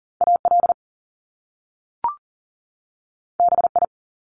Morse code, heard as a keyed tone, about 40 wpm, 700 Hz. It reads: AL BI
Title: Top Two Letter Combinations in Words 40wpm